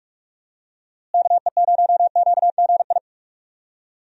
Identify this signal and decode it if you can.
Morse code, heard as a keyed tone, about 45 wpm, 700 Hz. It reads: KE0QGI
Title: USA Call Signs - 6 Characters - Repeated 2x - 45wpm